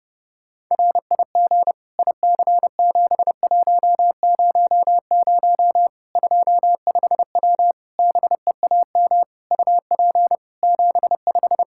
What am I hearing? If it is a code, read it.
Morse code, heard as a keyed tone, about 32 wpm, 700 Hz. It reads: RIG IC7100 25W BEAM UP 75
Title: Practice QSO - Exchange 2 - Info - 30wpm